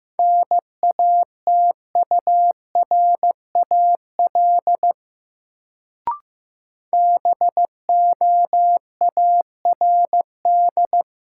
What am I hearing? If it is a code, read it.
Morse code, heard as a keyed tone, about 15 wpm, 700 Hz. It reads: NATURAL BOARD
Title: Sentences from Top 700 Words - New Word Drill - 15wpm